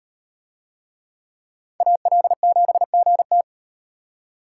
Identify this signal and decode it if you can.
Morse code, heard as a keyed tone, about 38 wpm, 700 Hz. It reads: AL7GT